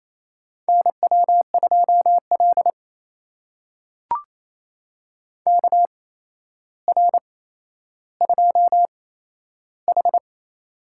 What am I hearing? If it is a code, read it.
Morse code, heard as a keyed tone, about 28 wpm, spaced as for 12 wpm, 700 Hz. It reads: NW2L KR2H
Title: USA Call Signs - 4 Characters - 8x Character Spacing - 28wpm